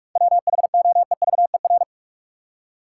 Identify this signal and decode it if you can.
Morse code, heard as a keyed tone, about 45 wpm, 700 Hz. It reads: WHOEVER
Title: Seven Letter Words - 45wpm